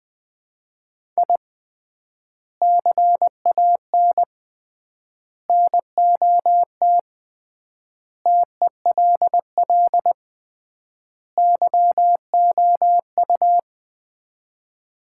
Morse code, heard as a keyed tone, about 20 wpm, 700 Hz. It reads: I CAN NOT TELL YOU
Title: 3x Word-Spacing - Sets of 5 Words - 20wpm